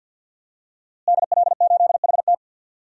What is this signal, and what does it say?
Morse code, heard as a keyed tone, about 49 wpm, 700 Hz. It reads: DL8HT
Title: DXCC-100 Call Signs - 5 Characters - Repeated 3x - 50wpm